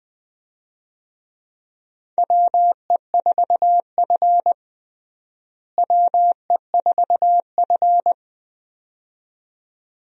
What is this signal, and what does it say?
Morse code, heard as a keyed tone, about 20 wpm, 700 Hz. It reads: WE4F WE4F
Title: USA Call Signs - 4 Characters - Repeated 2x - 20wpm